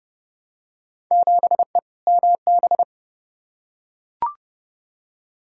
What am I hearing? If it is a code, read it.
Morse code, heard as a keyed tone, about 30 wpm, 700 Hz. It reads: 7E MB
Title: Field Day Exchanges Encore - 30wpm